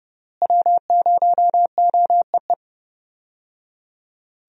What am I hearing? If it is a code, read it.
Morse code, heard as a keyed tone, about 30 wpm, 700 Hz. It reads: W0OEE
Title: USA Call Signs - 5 Characters - Repeated 3x - 30wpm